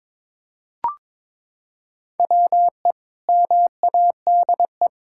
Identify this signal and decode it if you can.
Morse code, heard as a keyed tone, about 22 wpm, 700 Hz. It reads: WE MADE
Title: Sentences from Top 1600 Words 22wpm